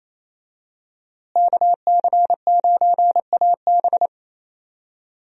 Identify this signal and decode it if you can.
Morse code, heard as a keyed tone, about 28 wpm, 700 Hz. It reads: KC9AB
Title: USA Call Signs - 5 Characters - Repeated 3x - 28wpm